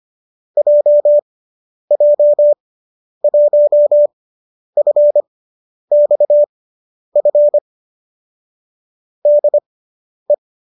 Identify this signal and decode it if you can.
Morse code, heard as a keyed tone, about 25 wpm, spaced as for 10 wpm, 570 Hz. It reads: JJ1FXF DE